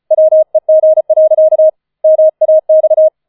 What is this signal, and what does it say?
Morse code, heard as a keyed tone, about 34 wpm, 630 Hz. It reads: WEG. MAX